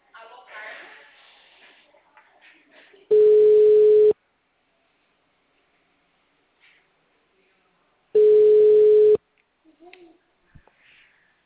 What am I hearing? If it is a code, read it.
Ringback tone (Europe, 425 Hz), with speech.